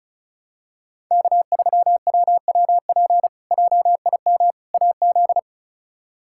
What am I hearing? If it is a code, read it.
Morse code, heard as a keyed tone, about 34 wpm, 700 Hz. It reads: K3WWP JIM AZ
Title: K1USN SST QSO Practice - 35wpm